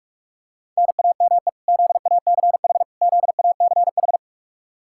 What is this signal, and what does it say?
Morse code, heard as a keyed tone, about 46 wpm, 700 Hz. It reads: NAME ZACH ZACH
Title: Practice QSO - Exchange 1 - Beginning - 45wpm